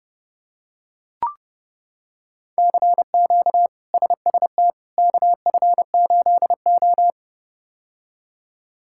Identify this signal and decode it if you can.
Morse code, heard as a keyed tone, about 31 wpm, 700 Hz. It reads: CQ SST KF8O
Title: K1USN SST QSO Practice - 30wpm